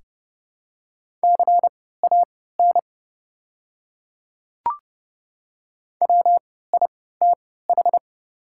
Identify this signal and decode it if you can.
Morse code, heard as a keyed tone, about 30 wpm, spaced as for 17 wpm, 700 Hz. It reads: CAN WITH
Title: 3x Character Spacing - Top 100 Words - 30wpm